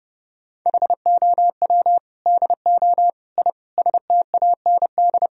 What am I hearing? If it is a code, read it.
Morse code, heard as a keyed tone, about 32 wpm, 700 Hz. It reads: HOW DO I STAND
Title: Sentences from Top 1000 Words - Review Part 1 - 30wpm